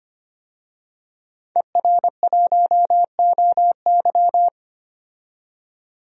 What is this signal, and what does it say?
Morse code, heard as a keyed tone, about 25 wpm, 700 Hz. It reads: ER1OY
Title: DXCC-100 Call Signs - 5 Characters - 25wpm